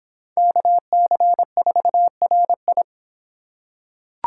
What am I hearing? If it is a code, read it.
Morse code, heard as a keyed tone, about 23 wpm, 700 Hz. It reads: KC4RI